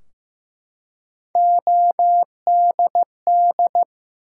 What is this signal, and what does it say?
Morse code, heard as a keyed tone, about 15 wpm, 700 Hz. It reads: ODD
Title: Three Letter Words - 15wpm